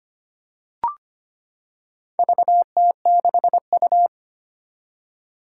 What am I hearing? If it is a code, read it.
Morse code, heard as a keyed tone, about 25 wpm, 700 Hz. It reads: VT6U
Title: Four Challenging Letter-Number Combinations 25wpm